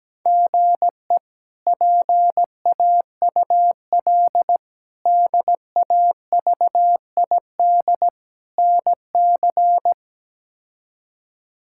Morse code, heard as a keyed tone, about 17 wpm, 700 Hz. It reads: GE PAUL DAVID NC
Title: K1USN SST QSO Practice - 17wpm